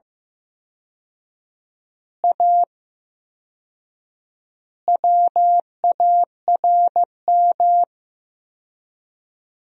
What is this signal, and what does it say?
Morse code, heard as a keyed tone, about 15 wpm, 700 Hz. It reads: A WARM
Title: 4x Word-Spacing - Sentences from Top 500 Words - 15wpm